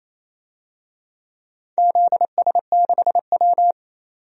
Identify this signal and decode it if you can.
Morse code, heard as a keyed tone, about 28 wpm, 700 Hz. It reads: ZS6W
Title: DXCC-100 Call Signs - 4 Characters - 28wpm